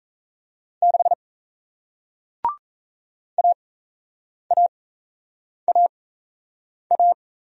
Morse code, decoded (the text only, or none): 6 AAAA